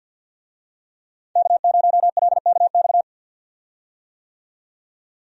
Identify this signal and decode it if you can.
Morse code, heard as a keyed tone, about 49 wpm, 700 Hz. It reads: K0LKX
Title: USA Call Signs - 5 Characters - 50wpm